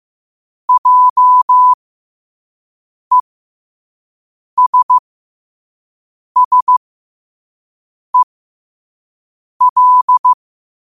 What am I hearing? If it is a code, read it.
Morse code, heard as a keyed tone, about 15 wpm, spaced as for 5 wpm, 1000 Hz. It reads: JESSEL
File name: qotdfiles / 5wpm